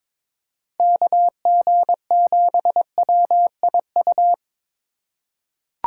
Morse code, heard as a keyed tone, about 22 wpm, 700 Hz. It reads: KG7WIU
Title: USA Call Signs - 6 Characters - 2x Character Spacing - 22wpm